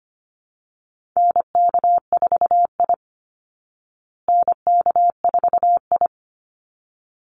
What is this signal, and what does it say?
Morse code, heard as a keyed tone, about 25 wpm, 700 Hz. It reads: NK4I NK4I
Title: Call Signs Repeated 3x - 25wpm